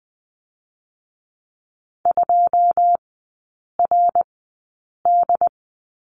Morse code, heard as a keyed tone, about 20 wpm, spaced as for 8 wpm, 700 Hz. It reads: 2RD
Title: 081 Lesson 14e - Number 2 - Three Characters 20wpm